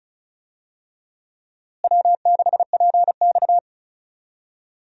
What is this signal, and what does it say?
Morse code, heard as a keyed tone, about 36 wpm, 700 Hz. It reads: W6PX
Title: USA Call Signs - 4 Characters - Repeated 3x - 35wpm